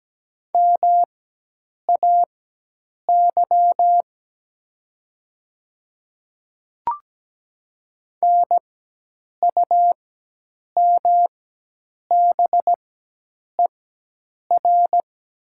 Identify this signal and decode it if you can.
Morse code, heard as a keyed tone, about 17 wpm, spaced as for 8 wpm, 700 Hz. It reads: MAY NUMBER
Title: 4x Character Spacing - Top 100 Words - 17wpm